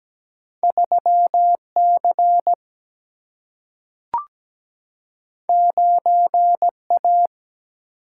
Morse code, heard as a keyed tone, about 17 wpm, 700 Hz. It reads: 3C 9A
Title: Single-Letter Single-Number 17wpm